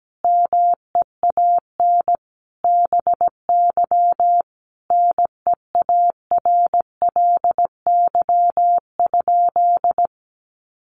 Morse code, heard as a keyed tone, about 17 wpm, 700 Hz. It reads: MEAN BY NEARLY?
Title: Sentences from Top 1300 Words 17wpm